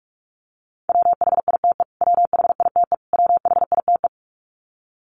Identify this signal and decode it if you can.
Morse code, heard as a keyed tone, about 44 wpm, 700 Hz. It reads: WHITE WHITE WHITE
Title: Top 500 Words Repeated 45wpm